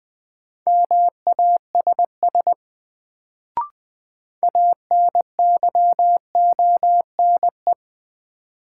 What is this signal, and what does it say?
Morse code, heard as a keyed tone, about 20 wpm, 700 Hz. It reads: MASS ANYONE
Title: Sets of 5 Words Encore - New Word Drill - 20wpm